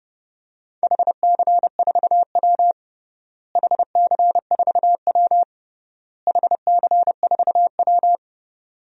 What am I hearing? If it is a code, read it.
Morse code, heard as a keyed tone, about 30 wpm, 700 Hz. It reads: HC4W HC4W HC4W
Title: DXCC-100 Call Signs - 4 Characters - Repeated 3x - 30wpm